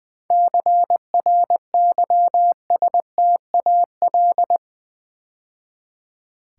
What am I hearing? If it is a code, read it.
Morse code, heard as a keyed tone, about 20 wpm, 700 Hz. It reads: CRYSTAL
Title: 7 Letter Words - First 3 Letters Given - 20wpm